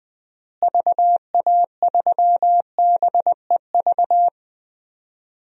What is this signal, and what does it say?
Morse code, heard as a keyed tone, about 19 wpm, 700 Hz. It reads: VA3BEV
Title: DXCC-100 Call Signs - 6 Characters - Repeated 3x - 20wpm